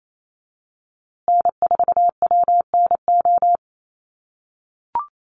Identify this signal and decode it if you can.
Morse code, heard as a keyed tone, about 28 wpm, 700 Hz. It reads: N4WNO
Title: USA Call Signs Repeated 2x 28wpm